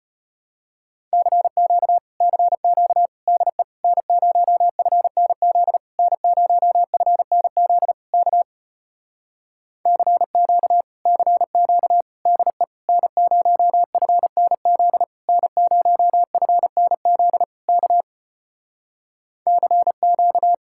Morse code, heard as a keyed tone, about 33 wpm, 700 Hz. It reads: CQ CQ DE N0FNZ N0FNZ K CQ CQ DE N0FNZ N0FNZ K CQ